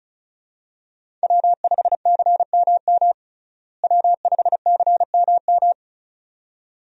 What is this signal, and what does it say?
Morse code, heard as a keyed tone, about 34 wpm, 700 Hz. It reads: W5CMM W5CMM
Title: USA Call Signs - 5 Characters - Repeated 2x - 35wpm